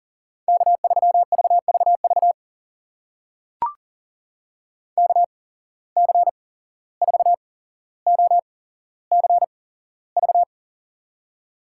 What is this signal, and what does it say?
Morse code, heard as a keyed tone, about 40 wpm, spaced as for 16 wpm, 700 Hz. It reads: K3VVV KC4OCV
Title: USA Call Signs - 8x Character Spacing - 40wpm